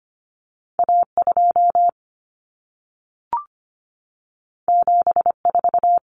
Morse code, heard as a keyed tone, about 25 wpm, 700 Hz. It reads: A2 74
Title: Common Cut Numbers - 2-Digit 25wpm